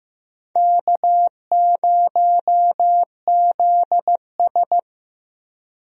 Morse code, heard as a keyed tone, about 15 wpm, 700 Hz. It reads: K0ZS
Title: USA Call Signs - 4 Characters - Repeated 2x - 15wpm